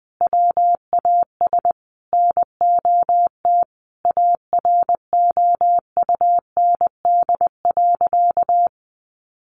Morse code, heard as a keyed tone, about 20 wpm, 700 Hz. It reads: WAS NOT AROUND.
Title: Sentences from Top 1300 Words 20wpm